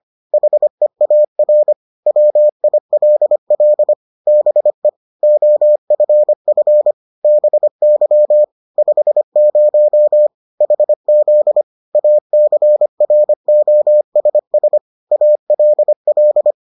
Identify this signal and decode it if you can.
Morse code, heard as a keyed tone, about 25 wpm, 600 Hz. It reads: HEAR WILL BE OFF BY 50 HZ ACROSS ALL